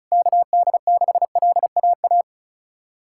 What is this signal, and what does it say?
Morse code, heard as a keyed tone, about 36 wpm, 700 Hz. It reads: KD6LAA
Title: USA Callsign DE Callsign - 6 Characters - 35wpm